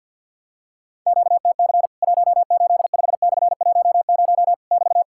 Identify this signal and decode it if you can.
Morse code, heard as a keyed tone, about 50 wpm, 700 Hz. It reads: QTC 185/10 <BT>
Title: Worked All Europe DX Contest - QTC Exchange - 50wpm